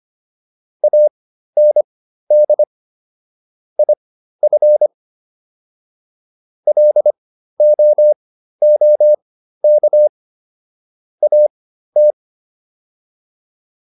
Morse code, heard as a keyed tone, about 25 wpm, spaced as for 13 wpm, 600 Hz. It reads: AND IF LOOK AT